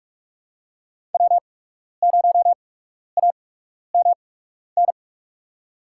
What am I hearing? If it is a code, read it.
Morse code, heard as a keyed tone, about 46 wpm, spaced as for 12 wpm, 700 Hz. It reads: W0AMN
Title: USA Call Signs - 5 Characters - 8x Character Spacing - 45wpm